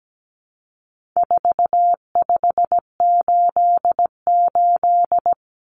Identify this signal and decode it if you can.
Morse code, heard as a keyed tone, about 17 wpm, 700 Hz. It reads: 4588
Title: Common Cut Numbers - 4-Digit 17wpm